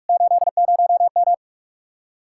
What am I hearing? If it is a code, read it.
Morse code, heard as a keyed tone, about 54 wpm, 700 Hz. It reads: 80M
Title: Speed-Racing Common QSO Phrases - 45wpm